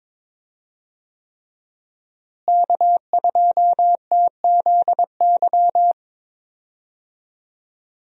Morse code, heard as a keyed tone, about 22 wpm, 700 Hz. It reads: K2TZY
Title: USA Call Signs - 5 Characters - Repeated 3x - 22wpm